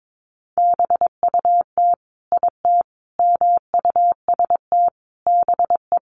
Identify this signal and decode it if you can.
Morse code, heard as a keyed tone, about 22 wpm, 700 Hz. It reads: BUT IT MUST BE
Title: Sentences from Top 600 Words 22wpm